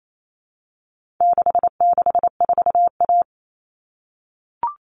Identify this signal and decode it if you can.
Morse code, heard as a keyed tone, about 28 wpm, 700 Hz. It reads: 664A
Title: All Cut Numbers - 4-Digit 28wpm